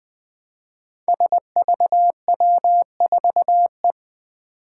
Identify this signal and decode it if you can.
Morse code, heard as a keyed tone, about 20 wpm, 700 Hz. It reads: SVW4E
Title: Five Challenging Letter-Number Combinations 20wpm